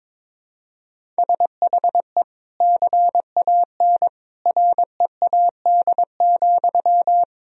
Morse code, heard as a keyed tone, about 22 wpm, 700 Hz. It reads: SHE CAN READ,
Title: Sentences from Top 1000 Words - Review Part 1 - 22wpm